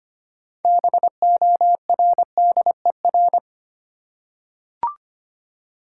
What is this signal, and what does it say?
Morse code, heard as a keyed tone, about 25 wpm, 700 Hz. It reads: BORDER
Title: Six Letter Words - 25wpm